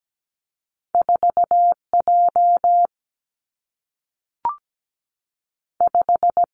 Morse code, heard as a keyed tone, about 17 wpm, 700 Hz. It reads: 4J 5
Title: Two Challenging Letter-Number Combinations 17wpm